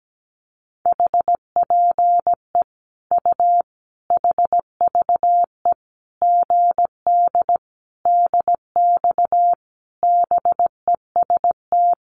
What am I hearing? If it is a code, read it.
Morse code, heard as a keyed tone, about 17 wpm, 700 Hz. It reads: HPE U HVE GD DX BEST
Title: Common QSO Phrases 17wpm